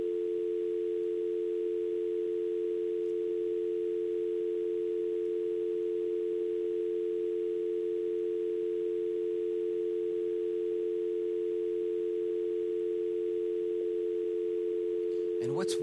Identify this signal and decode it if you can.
Telephone call progress tone (North America), heard as a telephone network's tone, about 350, 440 Hz, with speech. Dial tone.